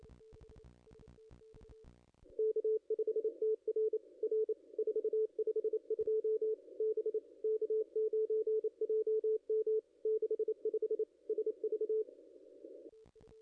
Morse code, heard as a keyed tone, about 29 wpm, 440 Hz. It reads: K5TR R 452 B K9JM 65 SV
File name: contests / ss-cw-k5tr-op-n5rz -0400